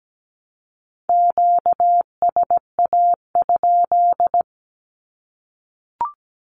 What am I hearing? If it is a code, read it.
Morse code, heard as a keyed tone, about 17 wpm, 700 Hz. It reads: QSA?
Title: Q-Codes Repeated 3x 17wpm